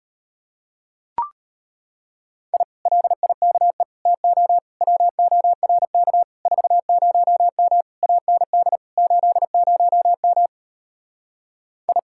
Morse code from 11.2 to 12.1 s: I

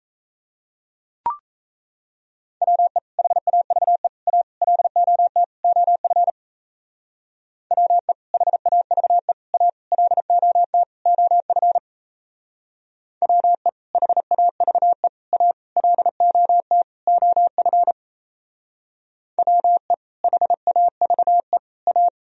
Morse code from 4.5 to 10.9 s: LOT OF WE HAVE A LOT

